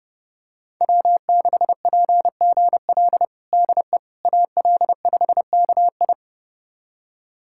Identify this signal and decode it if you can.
Morse code, heard as a keyed tone, about 32 wpm, 700 Hz. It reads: W6PGL DE AL5KI
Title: USA Callsign DE Callsign - 5 Characters - 30wpm